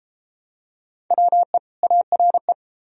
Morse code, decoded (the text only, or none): WE ARE